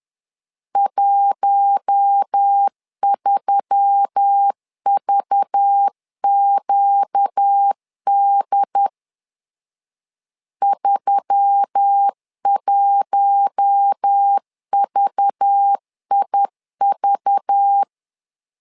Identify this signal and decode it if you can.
Morse code, heard as a keyed tone, about 10 wpm, 800 Hz. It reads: 13VQD 31VIV
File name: LEZIONI / 5 LBVC / 5 LBV 10wpm 2248